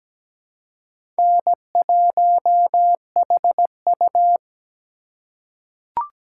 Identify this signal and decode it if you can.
Morse code, heard as a keyed tone, about 17 wpm, 700 Hz. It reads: N1HU
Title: USA Call Signs - 4 Characters - 3x Character Spacing - 17wpm